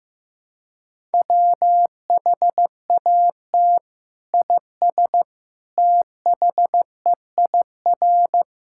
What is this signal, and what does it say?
Morse code, heard as a keyed tone, about 15 wpm, 700 Hz. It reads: WHAT IS THEIR
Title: Sentences from Top 1000 Words - Review Part 5 - 15wpm